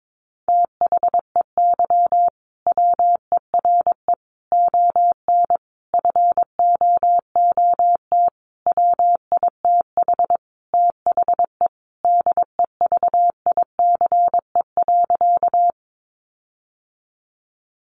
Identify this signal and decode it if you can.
Morse code, heard as a keyed tone, about 22 wpm, 700 Hz. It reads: THEY WERE ON FOOT WITH THE DEVICE.
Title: Sentences from Top 1500 Words 22wpm